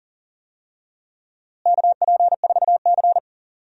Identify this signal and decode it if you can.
Morse code, heard as a keyed tone, about 40 wpm, 700 Hz. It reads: KP4C
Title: DXCC-100 Call Signs - 4 Characters - Repeated 3x - 40wpm